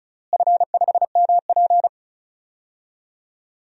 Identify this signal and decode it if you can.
Morse code, heard as a keyed tone, about 34 wpm, 700 Hz. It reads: F5MP
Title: DXCC-100 Call Signs - 4 Characters - Repeated 2x - 35wpm